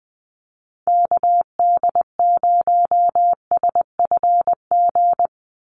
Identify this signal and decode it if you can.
Morse code, heard as a keyed tone, about 20 wpm, 700 Hz. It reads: KD0SFG
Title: USA Call Signs 20wpm